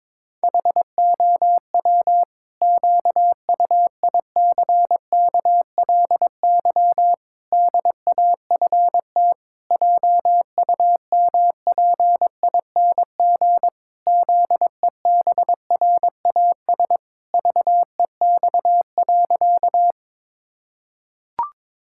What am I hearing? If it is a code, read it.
Morse code, heard as a keyed tone, about 22 wpm, 700 Hz. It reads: HOW QUICKLY DAFT JUMPING ZEBRAS VEX.
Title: Pangram Challenge v2 - 22wpm